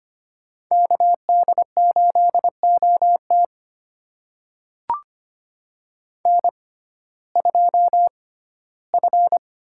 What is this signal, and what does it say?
Morse code, heard as a keyed tone, about 25 wpm, spaced as for 15 wpm, 700 Hz. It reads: KD8OT N2F
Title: USA Call Signs - 5 Characters - 6x Character Spacing - 25wpm